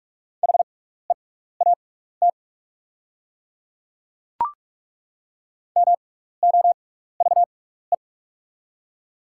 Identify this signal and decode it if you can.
Morse code, heard as a keyed tone, about 43 wpm, spaced as for 15 wpm, 690 Hz. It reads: HEAT MOVE
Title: 6x Character Spacing - Top 500 Words - 45wpm